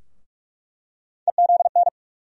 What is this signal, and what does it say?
Morse code, heard as a keyed tone, about 44 wpm, 700 Hz. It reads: EZN
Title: USA Call Sign Postfixes - 3 Letters 45wpm